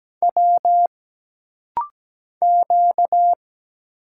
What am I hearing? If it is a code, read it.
Morse code, heard as a keyed tone, about 17 wpm, 700 Hz. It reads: W Q
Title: Single Letters - Rapid-Fire 17wpm